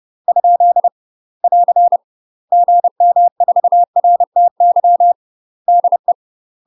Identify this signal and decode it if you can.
Morse code, heard as a keyed tone, about 30 wpm, 700 Hz. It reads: ? <AR> GM4RTY DE
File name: UK MASTER QSOs TRAINING / 30WPM / UK MASTER QSOs 0030